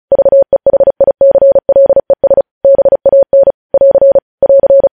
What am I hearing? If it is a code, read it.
Morse code, heard as a keyed tone, about 36 wpm, 550 Hz. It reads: VEHICLES BAN <AR> <AR>
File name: podcast / News headlines-1x-35x35